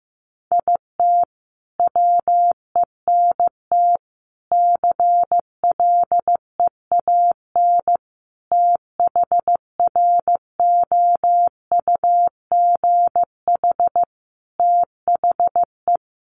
Morse code, heard as a keyed tone, about 15 wpm, 700 Hz. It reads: IT WENT CLEAN THROUGH THE
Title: Sentences from Top 1200 Words 15wpm